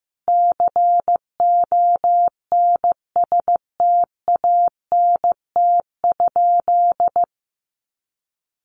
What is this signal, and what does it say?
Morse code, heard as a keyed tone, about 15 wpm, 700 Hz. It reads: CONSTANT?
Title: Sentences from Top 1500 Words 15wpm